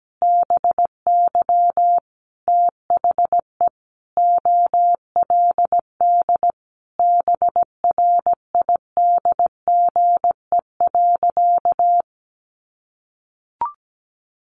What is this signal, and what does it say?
Morse code, heard as a keyed tone, about 17 wpm, 700 Hz. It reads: BY THE OLD BRIDGE.
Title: Sentences from Top 1200 Words 17wpm